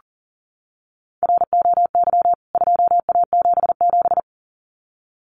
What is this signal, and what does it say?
Morse code, heard as a keyed tone, about 40 wpm, 700 Hz. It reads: ROY 2A77
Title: CWT Exchange 40wpm